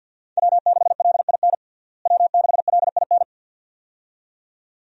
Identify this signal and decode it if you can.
Morse code, heard as a keyed tone, about 46 wpm, 700 Hz. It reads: W6LIN W6LIN
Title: USA Call Signs - 5 Characters - Repeated 2x - 50wpm